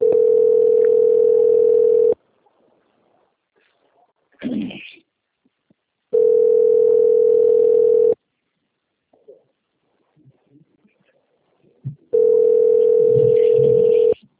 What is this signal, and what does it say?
Ringback tone (North America), 440 + 480 Hz.